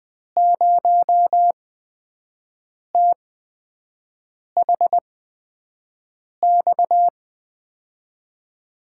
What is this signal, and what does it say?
Morse code, heard as a keyed tone, about 20 wpm, spaced as for 5 wpm, 700 Hz. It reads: 0THX
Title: USA Call Signs - 6 Characters - 8x Character Spacing - 20wpm